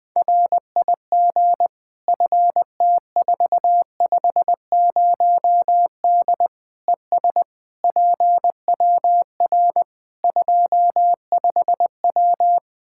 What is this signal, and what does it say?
Morse code, heard as a keyed tone, about 20 wpm, 700 Hz. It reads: RIG FT450D ES PWR 25W
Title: Practice QSO - Exchange 2 - Info - 20wpm